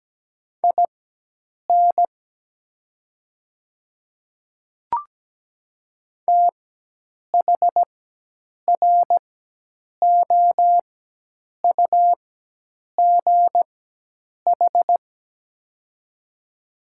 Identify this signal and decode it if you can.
Morse code, heard as a keyed tone, about 17 wpm, spaced as for 8 wpm, 700 Hz. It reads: IN THROUGH
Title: 4x Character Spacing - Top 200 Words - 17wpm